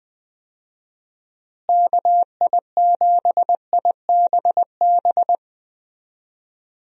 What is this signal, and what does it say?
Morse code, heard as a keyed tone, about 20 wpm, 700 Hz. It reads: KI7IBB
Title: USA Call Signs - 6 Characters - Repeated 2x - 20wpm